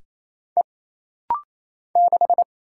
Morse code, heard as a keyed tone, about 27 wpm, 700 Hz. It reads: E 6
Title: Single Letter-Number - Rapid-Fire 28wpm